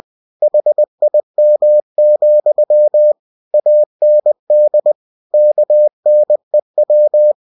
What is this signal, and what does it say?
Morse code, heard as a keyed tone, about 20 wpm, 600 Hz. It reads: HIM, AND KNEW
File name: cw-books / winnie-the-pooh / winnie-20wpm-page-0096